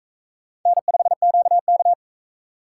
Morse code, heard as a keyed tone, about 41 wpm, 700 Hz. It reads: N5QK